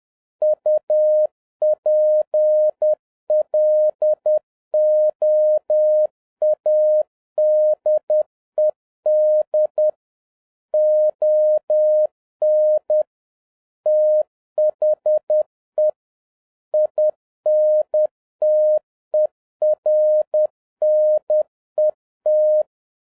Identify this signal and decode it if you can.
Morse code, heard as a keyed tone, about 10 wpm, 610 Hz. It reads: UPLOADED ON THE INTERNET
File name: 2011-07-mp3 / n31 THE 3.11 2 10wpm